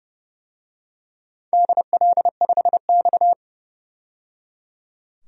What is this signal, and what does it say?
Morse code, heard as a keyed tone, about 30 wpm, 700 Hz. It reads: DL5X
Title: DXCC-100 Call Signs - 4 Characters - 30wpm